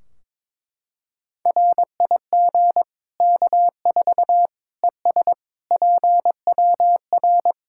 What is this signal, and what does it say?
Morse code, heard as a keyed tone, about 21 wpm, 700 Hz. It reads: RIG K4 ES PWR
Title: Practice QSO - Exchange 2 - Info - 22wpm